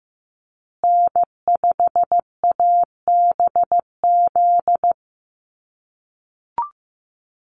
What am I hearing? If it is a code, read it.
Morse code, heard as a keyed tone, about 15 wpm, 700 Hz. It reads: N5ABZ
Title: USA Call Signs Repeated 2x 15wpm